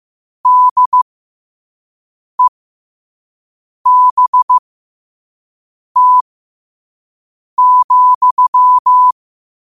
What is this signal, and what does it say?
Morse code, heard as a keyed tone, about 15 wpm, spaced as for 5 wpm, 1000 Hz. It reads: DEBT,
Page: New quotes every day in morse code at 5 Words per minute.